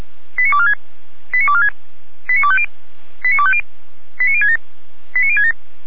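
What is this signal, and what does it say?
Five-tone selective call (ZVEI), heard as quick runs of five tones, tones of about 70 ms. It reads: 89246 89246 89250 89250 89067 89067
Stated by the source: ZVEI five-tone call received on SDR, from the feuerwehr-orlamuende.org fire brigade audio archive.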